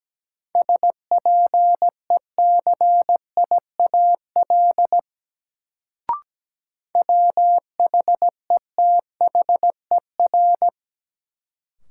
Morse code, heard as a keyed tone, about 17 wpm, 700 Hz. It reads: SPECIAL WHETHER
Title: Sentences from Top 1000 Words - Review Part 3 - New Word Drill - 17wpm